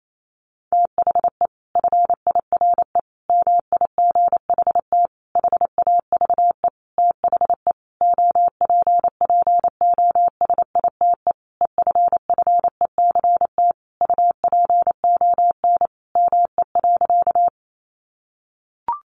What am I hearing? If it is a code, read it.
Morse code, heard as a keyed tone, about 28 wpm, 700 Hz. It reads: THE FIRE MIGHT HAVE THE OPPOSITE EFFECT UPON ME.
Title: Sentences from Top 1300 Words 28wpm